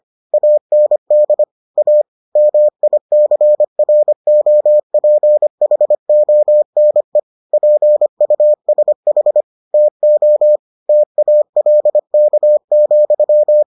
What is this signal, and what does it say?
Morse code, heard as a keyed tone, about 25 wpm, 600 Hz. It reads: AND A MICROPHONE PUSH TO TALK,